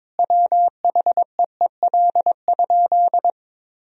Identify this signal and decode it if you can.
Morse code, heard as a keyed tone, about 23 wpm, 700 Hz. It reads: WHEEL?
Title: Sentences from Top 1000 Words - Review Part 3 - 22wpm